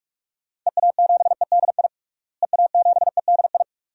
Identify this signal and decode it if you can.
Morse code, heard as a keyed tone, about 42 wpm, 700 Hz. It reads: EA7EDI EA7EDI
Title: DXCC-100 Call Signs - 6 Characters - Repeated 2x - 45wpm